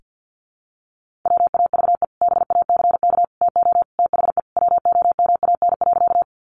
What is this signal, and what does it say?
Morse code, heard as a keyed tone, about 50 wpm, 700 Hz. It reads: WAVE BACK TO THE WOMAN.
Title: Sentences from Top 1400 Words 50wpm